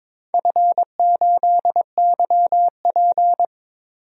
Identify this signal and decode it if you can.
Morse code, heard as a keyed tone, about 22 wpm, 700 Hz. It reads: F8YP
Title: DXCC-100 Call Signs - 4 Characters - Repeated 2x - 22wpm